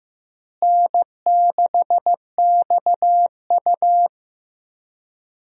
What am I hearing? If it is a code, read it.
Morse code, heard as a keyed tone, about 15 wpm, 700 Hz. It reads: N6XU
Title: USA Call Signs - 4 Characters - Repeated 3x - 15wpm